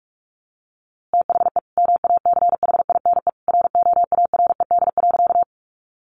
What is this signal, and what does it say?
Morse code, heard as a keyed tone, about 45 wpm, 700 Hz. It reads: THE MACHINE ROARED.
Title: A Sound Of Thunder 45wpm